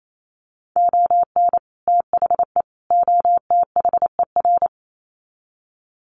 Morse code, heard as a keyed tone, about 28 wpm, 700 Hz. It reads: ON THE OTHER
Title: Sets of 3 Words 28wpm